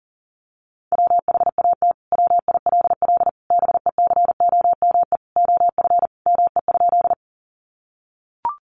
Morse code, heard as a keyed tone, about 40 wpm, 700 Hz. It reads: WHAT WILL BECOME OF ME?
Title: Sentences from Top 600 Words 40wpm